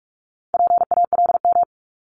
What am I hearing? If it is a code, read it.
Morse code, heard as a keyed tone, about 46 wpm, 700 Hz. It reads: PALM
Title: Multi-letter Phonemes Repeated 3x 45wpm